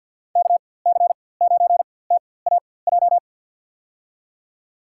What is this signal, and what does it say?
Morse code, heard as a keyed tone, about 52 wpm, spaced as for 23 wpm, 700 Hz. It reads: KC9TAJ
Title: USA Call Signs - 6 Characters - 4x Character Spacing - 50wpm